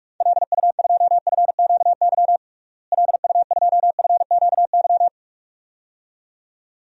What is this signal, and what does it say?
Morse code, heard as a keyed tone, about 42 wpm, 700 Hz. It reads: LU2FQY LU2FQY
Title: DXCC-100 Call Signs - 6 Characters - Repeated 2x - 45wpm